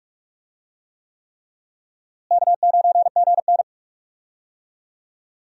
Morse code, heard as a keyed tone, about 46 wpm, 700 Hz. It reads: K9GN
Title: USA Call Signs - 4 Characters - Repeated 3x - 45wpm